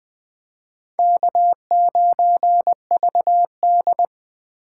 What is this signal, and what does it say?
Morse code, heard as a keyed tone, about 20 wpm, 700 Hz. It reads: K9VD
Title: 2023 SOTA Chaser Call Signs - 20wpm